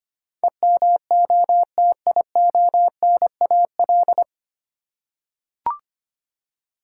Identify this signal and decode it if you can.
Morse code, heard as a keyed tone, about 25 wpm, 700 Hz. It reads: EMOTIONAL
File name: Nine Letter Words - 25wpm